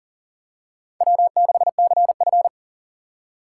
Morse code, heard as a keyed tone, about 40 wpm, 700 Hz. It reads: W6CF
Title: USA Call Signs - 4 Characters - 2x Character Spacing - 40wpm